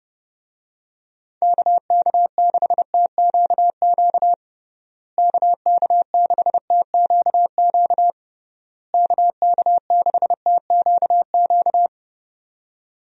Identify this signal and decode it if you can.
Morse code, heard as a keyed tone, about 32 wpm, 700 Hz. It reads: KK6TQQ KK6TQQ KK6TQQ
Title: USA Call Signs - 6 Characters - Repeated 3x - 30wpm